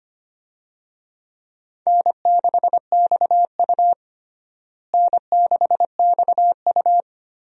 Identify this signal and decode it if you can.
Morse code, heard as a keyed tone, about 25 wpm, 700 Hz. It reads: N6XU N6XU
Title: USA Call Signs - 4 Characters - Repeated 3x - 25wpm